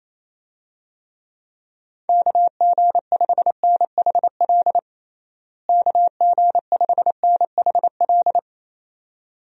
Morse code, heard as a keyed tone, about 28 wpm, 700 Hz. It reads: KG5NHL KG5NHL
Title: USA Call Signs - 6 Characters - Repeated 2x - 28wpm